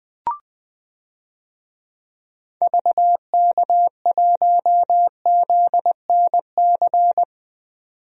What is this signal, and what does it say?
Morse code, heard as a keyed tone, about 20 wpm, 700 Hz. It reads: VK1ZNC
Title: DXCC-100 Call Signs - 6 Characters - 20wpm